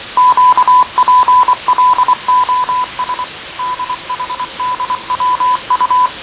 Morse code, heard as a keyed tone, about 24 wpm, 1000 Hz. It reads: QPLOS DHDWU
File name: fav22-6824usb-20251029-1420z-voice-test-win10